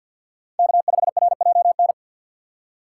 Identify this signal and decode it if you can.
Morse code, heard as a keyed tone, about 52 wpm, 700 Hz. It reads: K5RJN
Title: USA Call Signs - 4x Character Spacing - 50wpm